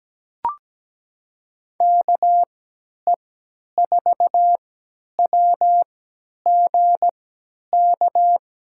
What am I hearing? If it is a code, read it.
Morse code, heard as a keyed tone, about 17 wpm, spaced as for 10 wpm, 700 Hz. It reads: KE4WGK
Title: USA Call Signs - 3x Character Spacing - 17wpm